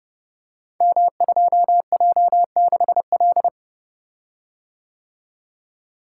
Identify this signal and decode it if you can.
Morse code, heard as a keyed tone, about 29 wpm, 700 Hz. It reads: M2J6L
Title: Code Groups - 5 Letter:Numbers - 30wpm